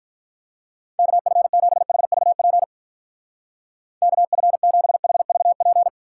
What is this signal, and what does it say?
Morse code, heard as a keyed tone, about 45 wpm, 700 Hz. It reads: KF7HVP KF7HVP